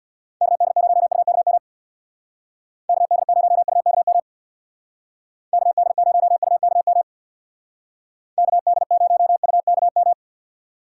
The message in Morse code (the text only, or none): KD0UKM KD0UKM KD0UKM KD0UKM